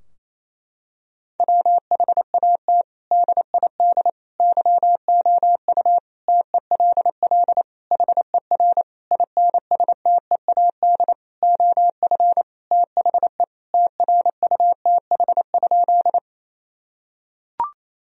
Morse code, heard as a keyed tone, about 28 wpm, 700 Hz. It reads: WHAT DID YOU TELL HER INSTEAD OF THE TRUTH?
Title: Sentences from Top 1000 Words - Review Part 4 - 28wpm